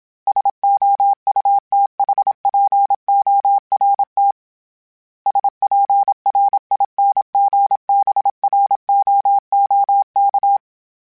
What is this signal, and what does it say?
Morse code, heard as a keyed tone, about 27 wpm, 800 Hz. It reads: SOUTHPORT SPRINGBROOK
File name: fromSMs / FromSMs mp3s / Cities@25